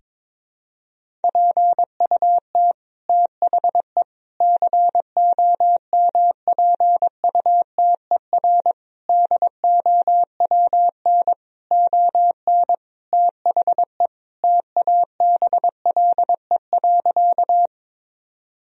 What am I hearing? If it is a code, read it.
Morse code, heard as a keyed tone, about 21 wpm, 700 Hz. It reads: PUT THE COMPUTER DOWN ON THE TABLE.
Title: Sentences from Top 1000 Words - Review Part 3 - 22wpm